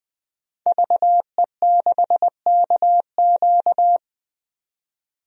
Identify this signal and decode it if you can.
Morse code, heard as a keyed tone, about 21 wpm, 700 Hz. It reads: VE6KQ
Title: DXCC-100 Call Signs - 5 Characters - 20wpm